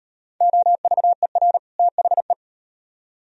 Morse code, heard as a keyed tone, about 37 wpm, 700 Hz. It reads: OVER THE